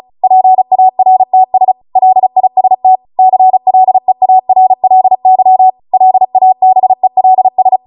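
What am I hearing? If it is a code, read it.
Morse code, heard as a keyed tone, about 36 wpm, 750 Hz. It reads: PARTS LIST CLEARLY LABELS
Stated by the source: Amateur radio Morse code practice archive transmission at 35 wpm.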